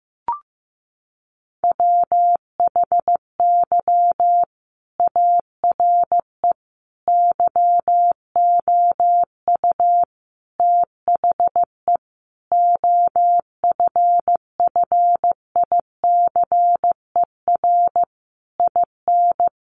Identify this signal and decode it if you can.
Morse code, heard as a keyed tone, about 15 wpm, 700 Hz. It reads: WHY ARE YOU THE OFFICER IN
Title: Sentences from Top 1200 Words 15wpm